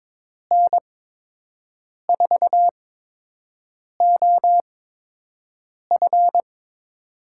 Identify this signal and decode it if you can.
Morse code, heard as a keyed tone, about 22 wpm, spaced as for 6 wpm, 700 Hz. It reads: N4OF
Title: USA Call Signs - 5 Characters - 8x Character Spacing - 22wpm